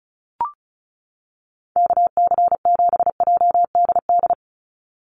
Morse code, heard as a keyed tone, about 36 wpm, 700 Hz. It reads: KC7JDD
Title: USA Call Signs 35wpm